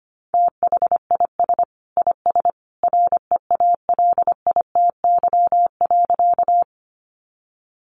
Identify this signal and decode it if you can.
Morse code, heard as a keyed tone, about 25 wpm, 700 Hz. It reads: THIS IS REALITY.
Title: Sentences from Top 1200 Words 25wpm